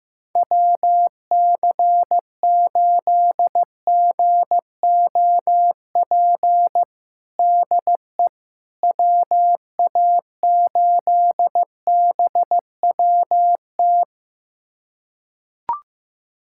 Morse code, heard as a keyed tone, about 16 wpm, 700 Hz. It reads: WC8GOP DE WA8BWT
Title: USA Callsign DE Callsign 15wpm